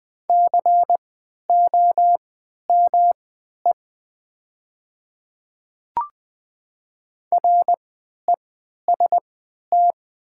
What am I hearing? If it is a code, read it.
Morse code, heard as a keyed tone, about 20 wpm, spaced as for 11 wpm, 690 Hz. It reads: COME REST
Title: 3x Character Spacing - Top 500 Words - 20wpm